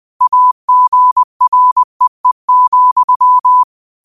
Morse code, heard as a keyed tone, about 20 wpm, 1000 Hz. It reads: AGREE,
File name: 20wpm